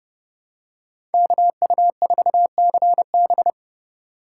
Morse code, heard as a keyed tone, about 30 wpm, 700 Hz. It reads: KU4CB
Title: USA Call Signs - 5 Characters - 3x Character Spacing - 30wpm